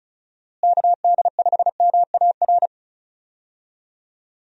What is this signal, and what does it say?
Morse code, heard as a keyed tone, about 34 wpm, 700 Hz. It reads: KD5MAR
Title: USA Call Signs - 6 Characters - Repeated 3x - 35wpm